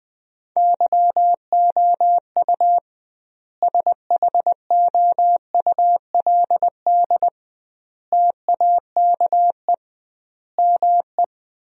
Morse code, heard as a keyed tone, about 20 wpm, 700 Hz. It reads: YOU SHOULD TAKE ME
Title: 2x Word-Spacing - Sentences from Top 300 Words - 20wpm